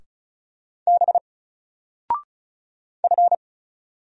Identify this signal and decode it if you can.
Morse code, heard as a keyed tone, about 36 wpm, 700 Hz. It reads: B F
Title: Single Letter-Number 35wpm